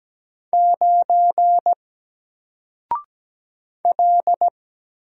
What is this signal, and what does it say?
Morse code, heard as a keyed tone, about 17 wpm, 700 Hz. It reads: 9 L
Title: Single Letter-Number Alt 17wpm